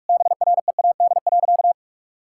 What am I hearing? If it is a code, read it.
Morse code, heard as a keyed tone, about 46 wpm, 700 Hz. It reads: BREAD.
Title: Sentences from Top 1600 Words 45wpm